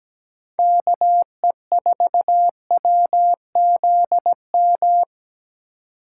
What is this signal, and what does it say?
Morse code, heard as a keyed tone, about 20 wpm, 700 Hz. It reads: KE4WZM
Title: Speed Racing - USA Call Signs 17wpm